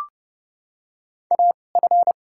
Morse code, heard as a keyed tone, about 30 wpm, spaced as for 22 wpm, 700 Hz. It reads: AF
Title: 2x Character Spacing - Top 200 Words - 30wpm